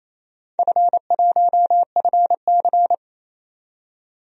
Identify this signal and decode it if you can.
Morse code, heard as a keyed tone, about 27 wpm, 700 Hz. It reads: F1FC
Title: DXCC-100 Call Signs - 4 Characters - 28wpm